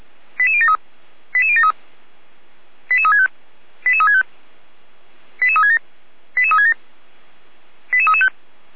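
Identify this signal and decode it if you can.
Five-tone selective call (ZVEI), heard as quick runs of five tones, tones of about 70 ms. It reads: 88073 88073 88365 88365 88367 88367 88335
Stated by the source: ZVEI five-tone selective call 89108, received via SDR, feuerwehr-orlamuende fire brigade.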